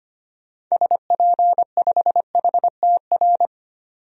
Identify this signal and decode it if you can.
Morse code, heard as a keyed tone, about 25 wpm, 700 Hz. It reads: SP5HTR
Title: DXCC-100 Call Signs - 6 Characters - 25wpm